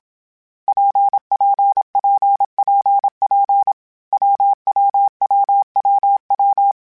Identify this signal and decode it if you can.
Morse code, heard as a keyed tone, about 27 wpm, 800 Hz. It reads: PPPPP WWWWW
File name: Difficult Combos / Difficult Combos-mp3s / WPJ WPJ1 @25